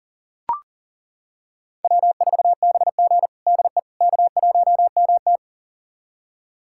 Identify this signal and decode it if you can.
Morse code, heard as a keyed tone, about 37 wpm, 700 Hz. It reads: W4BG DE K1MT
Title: USA Callsign DE Callsign - 4 Characters - 40wpm